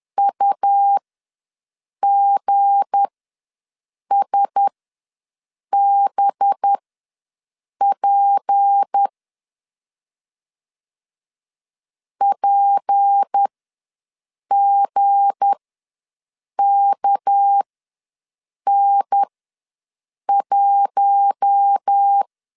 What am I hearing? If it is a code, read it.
Morse code, heard as a keyed tone, about 10 wpm, spaced as for 6 wpm, 800 Hz. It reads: UGSBP PGKN1
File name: LEZIONI / 7 GF9Z / 7 G ws 10wpm 3843